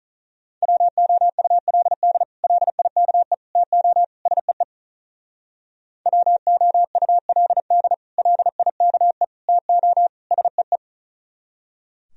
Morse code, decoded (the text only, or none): WOULD LIKE TO SEE WOULD LIKE TO SEE